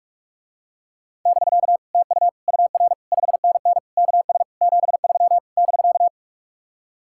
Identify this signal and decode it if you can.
Morse code, heard as a keyed tone, about 49 wpm, 700 Hz. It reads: <BK> TU UR 5NN KS 73 <BK>